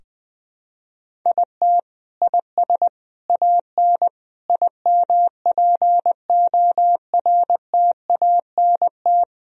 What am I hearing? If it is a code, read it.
Morse code, heard as a keyed tone, about 21 wpm, 700 Hz. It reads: IT IS AN IMPORTANT
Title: Sentences from Top 1000 Words - Review Part 1 - 20wpm